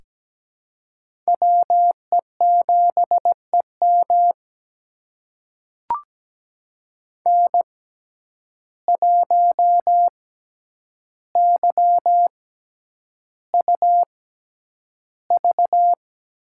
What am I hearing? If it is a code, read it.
Morse code, heard as a keyed tone, about 17 wpm, spaced as for 9 wpm, 700 Hz. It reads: WE7EM N1YUV
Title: USA Call Signs - 5 Characters - 6x Character Spacing - 17wpm